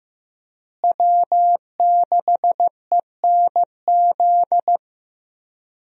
Morse code, heard as a keyed tone, about 15 wpm, 700 Hz. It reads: W6ENZ
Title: USA Call Signs - 5 Characters - Repeated 3x - 15wpm